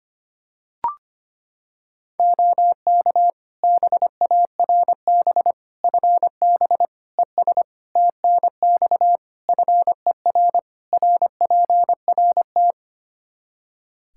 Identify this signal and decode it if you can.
Morse code, heard as a keyed tone, about 25 wpm, 700 Hz. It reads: OK BARB FB ES TNX FER RPRT
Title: Practice QSO - Exchange 2 - Info - 25wpm